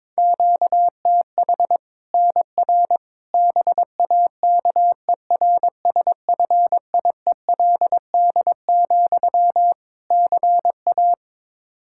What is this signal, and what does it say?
Morse code, heard as a keyed tone, about 22 wpm, 700 Hz. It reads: QTH NR BAKERSFIELD, CA
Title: K1USN SST QSO Practice 22wpm